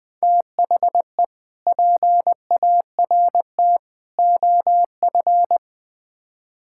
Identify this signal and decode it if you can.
Morse code, heard as a keyed tone, about 20 wpm, 700 Hz. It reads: THE PART OF
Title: Build A Phrase - 4-Word Phrases - 20wpm